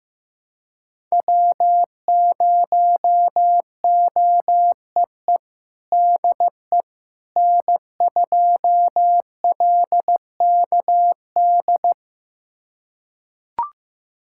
Morse code, heard as a keyed tone, about 15 wpm, 700 Hz. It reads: W0OEE DE N2LKD
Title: USA Callsign DE Callsign - 5 Characters - 15wpm